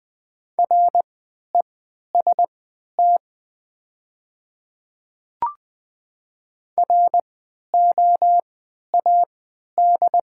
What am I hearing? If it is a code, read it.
Morse code, heard as a keyed tone, about 20 wpm, spaced as for 11 wpm, 700 Hz. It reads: REST ROAD
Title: 3x Character Spacing - Top 500 Words - 20wpm